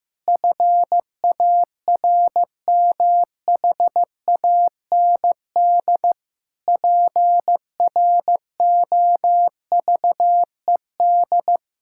Morse code, heard as a keyed tone, about 15 wpm, 700 Hz. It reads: FARMHAND PROVED
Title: Pangram Challenge v2 - 15wpm